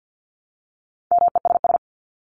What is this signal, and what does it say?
Morse code, heard as a keyed tone, about 50 wpm, 700 Hz. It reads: MESS
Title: Top 1751-2000 Words 50wpm